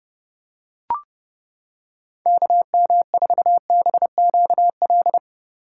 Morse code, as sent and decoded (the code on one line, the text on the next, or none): -.- -- ....- -... --.- .-..
KM4BQL